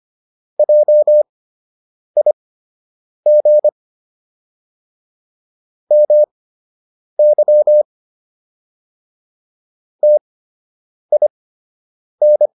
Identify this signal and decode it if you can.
Morse code, heard as a keyed tone, about 25 wpm, spaced as for 8 wpm, 600 Hz. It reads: JIG MY TIN